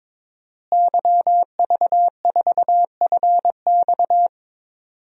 Morse code, heard as a keyed tone, about 22 wpm, 700 Hz. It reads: YV4FX
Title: DXCC-100 Call Signs - 5 Characters - Repeated 3x - 22wpm